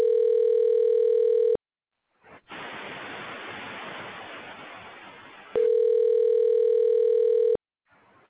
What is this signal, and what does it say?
Ringback tone (North America), 440 + 480 Hz.